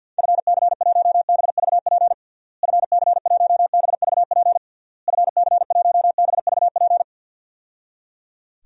Morse code, heard as a keyed tone, about 47 wpm, 700 Hz. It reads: FC1BVP FC1BVP FC1BVP
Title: DXCC-100 Call Signs - 6 Characters - Repeated 3x - 50wpm